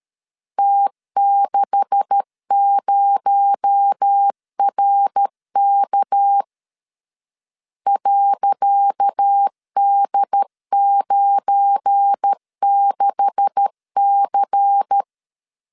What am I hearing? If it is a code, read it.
Morse code, heard as a keyed tone, about 13 wpm, 800 Hz. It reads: T60RK .D96C
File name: LEZIONI / 10 SIGNS / 10C DOT 12wpm 2451